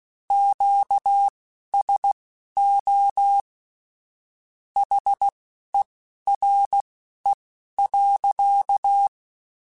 Morse code, heard as a keyed tone, about 16 wpm, spaced as for 11 wpm, 800 Hz. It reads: QSO HERE.
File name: QSOs / QSOs-mp3s / WD0BGP 2@10